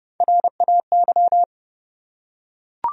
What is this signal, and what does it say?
Morse code, heard as a keyed tone, about 32 wpm, 700 Hz. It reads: RAY
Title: USA Common Names - Max 3 Letters - 30wpm